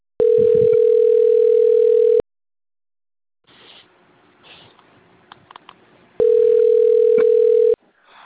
Ringback tone (North America), 440 + 480 Hz.